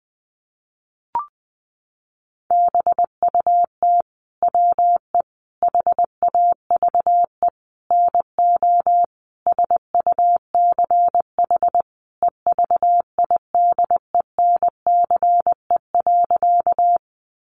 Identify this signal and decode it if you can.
Morse code, heard as a keyed tone, about 20 wpm, 700 Hz. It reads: BUT WE HAVE NO SUCH EVIDENCE.
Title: Sentences from Top 1300 Words 20wpm